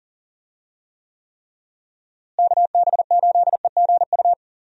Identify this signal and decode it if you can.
Morse code, heard as a keyed tone, about 40 wpm, 700 Hz. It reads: KB8EGU
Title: USA Call Signs - 6 Characters - Repeated 3x - 40wpm